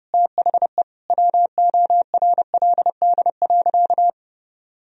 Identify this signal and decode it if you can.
Morse code, heard as a keyed tone, about 30 wpm, 700 Hz. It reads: THE WORLD.
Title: Sentences from Top 1000 Words - Review Part 3 - 30wpm